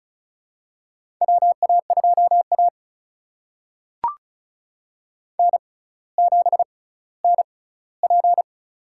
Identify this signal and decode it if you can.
Morse code, heard as a keyed tone, about 34 wpm, spaced as for 18 wpm, 700 Hz. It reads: WA2A N7NP
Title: USA Call Signs - 4 Characters - 6x Character Spacing - 35wpm